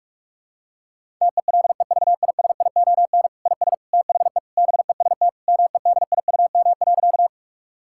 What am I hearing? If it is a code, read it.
Morse code, heard as a keyed tone, about 41 wpm, 700 Hz. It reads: TELEVISION IS THE BEST MEDIUM.
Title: Sentences from Top 1000 Words - Review Part 6 - 45wpm